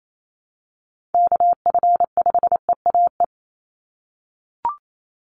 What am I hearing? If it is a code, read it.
Morse code, heard as a keyed tone, about 28 wpm, 700 Hz. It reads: KF5EAE
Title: USA Call Signs 28wpm